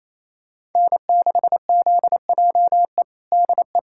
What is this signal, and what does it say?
Morse code, heard as a keyed tone, about 28 wpm, 700 Hz. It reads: N6ZJE DE
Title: USA Callsign DE Callsign - 5 Characters - 28wpm